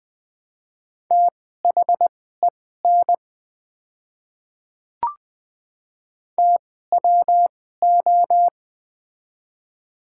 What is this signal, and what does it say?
Morse code, heard as a keyed tone, about 21 wpm, spaced as for 15 wpm, 700 Hz. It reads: THEN TWO
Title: 2x Character Spacing - Top 100 Words - 20wpm